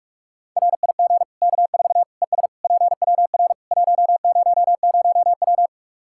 Morse code, heard as a keyed tone, about 46 wpm, 700 Hz. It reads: RIG K4 ES PWR 100W
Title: Practice QSO - Exchange 2 - Info - 45wpm